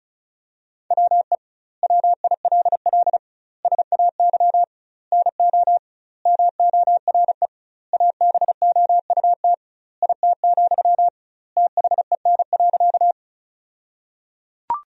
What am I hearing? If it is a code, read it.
Morse code, heard as a keyed tone, about 36 wpm, 700 Hz. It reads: WE WILL SAY NO MORE ABOUT IT, THEN.
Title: 2x Word-Spacing - Sentences from Top 200 Words - 35wpm